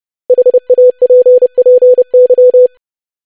Morse code, heard as a keyed tone, about 30 wpm, 500 Hz. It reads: HAPPY
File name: lesson / web / basic500-25wpm